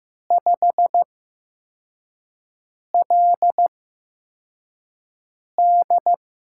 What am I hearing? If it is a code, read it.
Morse code, heard as a keyed tone, about 15 wpm, spaced as for 4 wpm, 700 Hz. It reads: 5LD
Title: USA Call Signs - 6 Characters - 8x Character Spacing - 15wpm